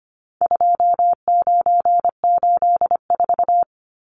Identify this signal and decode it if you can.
Morse code, heard as a keyed tone, about 25 wpm, 700 Hz. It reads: 2984
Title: Numbers - 4-Digit 25wpm